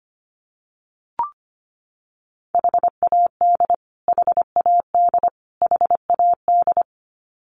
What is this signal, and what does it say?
Morse code, heard as a keyed tone, about 25 wpm, 700 Hz. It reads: HAD HAD HAD
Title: Top 500 Words Repeated 25wpm